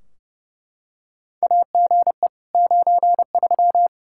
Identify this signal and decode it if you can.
Morse code, heard as a keyed tone, about 30 wpm, 700 Hz. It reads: AGE 93
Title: K1USN SST QSO Practice 30wpm